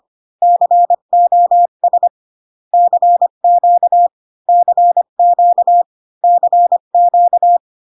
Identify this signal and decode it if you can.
Morse code, heard as a keyed tone, about 25 wpm, 700 Hz. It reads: COS CQ CQ CQ